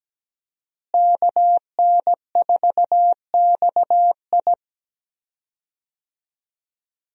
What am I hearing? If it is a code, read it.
Morse code, heard as a keyed tone, about 17 wpm, 700 Hz. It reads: KN4XI
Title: USA Call Signs - 5 Characters - Repeated 3x - 17wpm